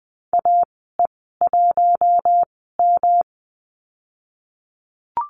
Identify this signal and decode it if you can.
Morse code, heard as a keyed tone, about 20 wpm, spaced as for 15 wpm, 700 Hz. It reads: AE1M
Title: 128 Lesson 21g - Number 6 - Call Signs 20wpm